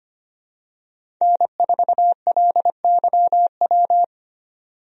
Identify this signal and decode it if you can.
Morse code, heard as a keyed tone, about 25 wpm, 700 Hz. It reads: N4LYW
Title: USA Call Signs - 5 Characters - 25wpm